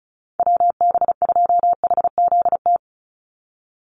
Morse code, heard as a keyed tone, about 34 wpm, 700 Hz. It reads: WB2HZT
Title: USA Call Signs 35wpm